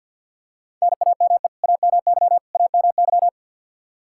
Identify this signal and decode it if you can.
Morse code, heard as a keyed tone, about 51 wpm, 700 Hz. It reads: NAME AMY AMY
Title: Practice QSO - Exchange 1 - Beginning - 50wpm